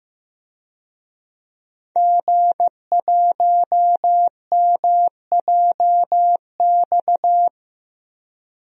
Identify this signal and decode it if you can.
Morse code, heard as a keyed tone, about 16 wpm, 700 Hz. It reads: G1MJX
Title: DXCC-100 Call Signs - 5 Characters - Repeated 3x - 15wpm